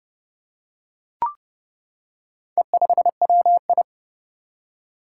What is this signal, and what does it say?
Morse code, heard as a keyed tone, about 30 wpm, 700 Hz. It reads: E5WI
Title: Four Challenging Letter-Number Combinations 30wpm